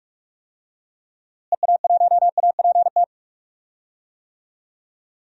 Morse code, heard as a keyed tone, about 44 wpm, 700 Hz. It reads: EA1APT
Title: DXCC-100 Call Signs - 6 Characters - Repeated 3x - 45wpm